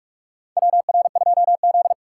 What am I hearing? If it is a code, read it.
Morse code, heard as a keyed tone, about 44 wpm, 690 Hz. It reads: WR2Z
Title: USA Call Signs - 4 Characters - 3x Character Spacing - 45wpm